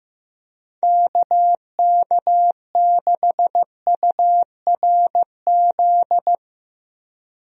Morse code, heard as a keyed tone, about 15 wpm, 700 Hz. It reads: KK6URZ
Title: USA Call Signs - 6 Characters - Repeated 3x - 15wpm